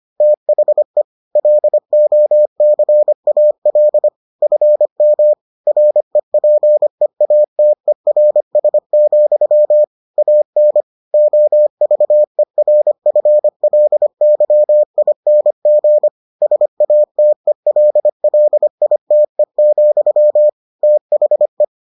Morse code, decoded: THE LOCAL FM REPEATERS, AN OVERFLYING SATELLITE, THE

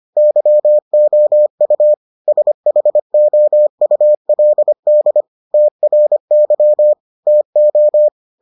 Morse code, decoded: YOU SHOULD TRY TO

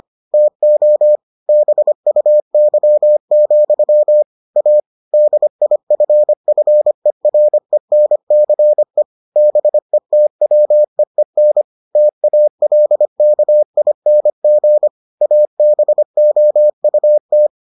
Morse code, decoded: TO BUY, A DIFFERENCE BETWEEN TALKING ABOUT

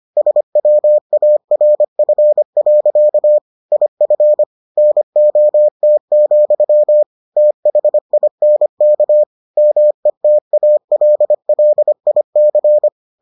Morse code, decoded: SWARF. IF NOT, THINK METALLIC

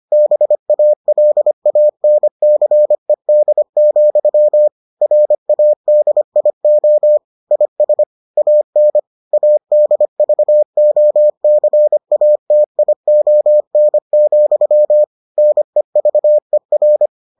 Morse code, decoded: BALANCED, RADIO IS AN ADVOCATION, NEVER

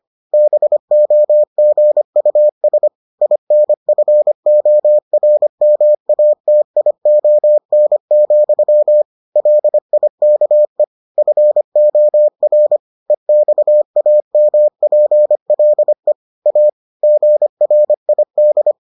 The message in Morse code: BOGUS INFORMATION, LIKE FOR EXAMPLE A GRID

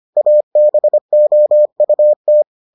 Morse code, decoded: ABOUT

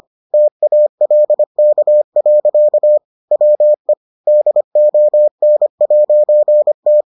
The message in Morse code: TALK. WE DON'T

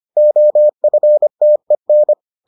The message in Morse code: OFTEN

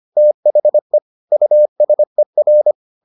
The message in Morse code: THE USER

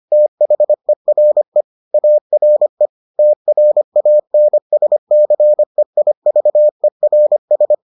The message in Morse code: THERE ARE TRANSCEIVERS

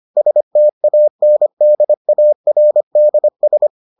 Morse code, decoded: STANDARDS

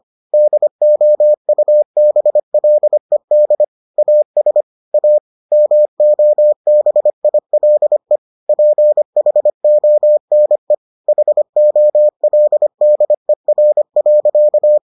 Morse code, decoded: DOUBLED AS A MOBILE PHONE HOLDER.

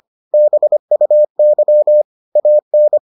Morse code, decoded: BUY AN

